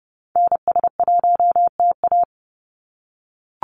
Morse code, heard as a keyed tone, about 30 wpm, 700 Hz. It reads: NS1TA
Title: NA SOTA Activator Callsigns 30wpm